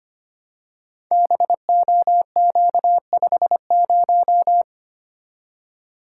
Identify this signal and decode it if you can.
Morse code, heard as a keyed tone, about 25 wpm, 700 Hz. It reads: BOQ50
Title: Code Groups - 5 Letter:Numbers - 25wpm